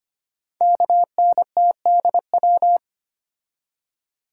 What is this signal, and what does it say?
Morse code, heard as a keyed tone, about 25 wpm, 700 Hz. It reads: KNTDW
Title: Code Groups - 5 Letters - 25wpm